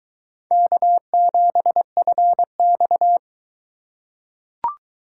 Morse code, heard as a keyed tone, about 22 wpm, 700 Hz. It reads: K7FX